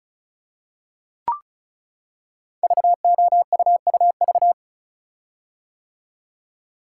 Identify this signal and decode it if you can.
Morse code, heard as a keyed tone, about 36 wpm, 690 Hz. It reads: VOUUV
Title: Code Groups - 5 Letters - 35wpm